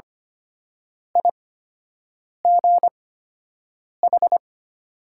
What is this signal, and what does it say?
Morse code, heard as a keyed tone, about 25 wpm, spaced as for 7 wpm, 700 Hz. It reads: IGH
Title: 8x Character Spacing - Top 300 Words - 25wpm